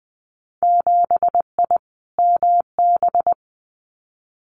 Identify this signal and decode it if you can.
Morse code, heard as a keyed tone, about 20 wpm, 700 Hz. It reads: 7I MB
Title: Winter Field Day Exchanges Encore 20wpm